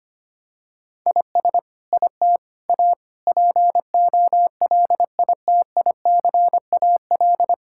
Morse code, heard as a keyed tone, about 25 wpm, 700 Hz. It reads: IS IT A POLITICAL
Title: Sentences from Top 1000 Words - Review Part 5 - 25wpm